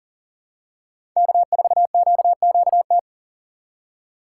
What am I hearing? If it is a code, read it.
Morse code, heard as a keyed tone, about 40 wpm, 700 Hz. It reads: K4QQT
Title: USA Call Signs - 5 Characters - 2x Character Spacing - 40wpm